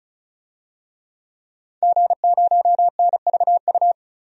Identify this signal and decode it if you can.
Morse code, heard as a keyed tone, about 34 wpm, 700 Hz. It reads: G0NVU
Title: DXCC-100 Call Signs - 5 Characters - 35wpm